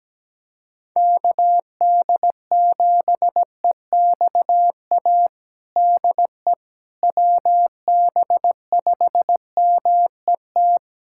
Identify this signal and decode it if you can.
Morse code, heard as a keyed tone, about 17 wpm, 700 Hz. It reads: KD7EXA DE WB5MET
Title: USA Callsign DE Callsign - 6 Characters - 17wpm